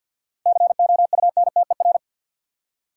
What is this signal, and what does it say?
Morse code, heard as a keyed tone, about 53 wpm, 700 Hz. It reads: COUNTER
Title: Seven Letter Words - 50wpm